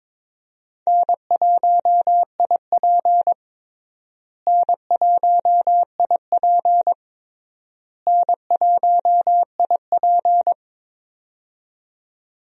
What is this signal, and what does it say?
Morse code, heard as a keyed tone, about 22 wpm, 700 Hz. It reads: N1IP N1IP N1IP
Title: USA Call Signs - 4 Characters - Repeated 3x - 22wpm